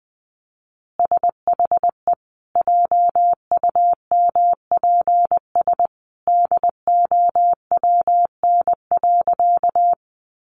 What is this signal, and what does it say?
Morse code, heard as a keyed tone, about 20 wpm, 700 Hz. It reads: SHE JUMPS DOWN.
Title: Easy Phrases 20wpm